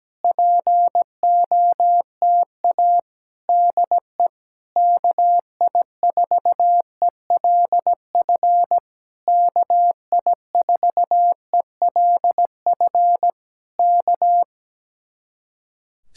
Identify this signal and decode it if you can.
Morse code, heard as a keyed tone, about 17 wpm, 700 Hz. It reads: POTA DE KI4ELF KI4ELF K
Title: Speed-Racing Common QSO Phrases - 17wpm